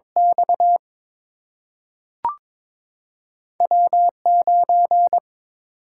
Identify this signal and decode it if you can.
Morse code, heard as a keyed tone, about 22 wpm, 700 Hz. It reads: X W 9
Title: Single-Letter Single-Number 22wpm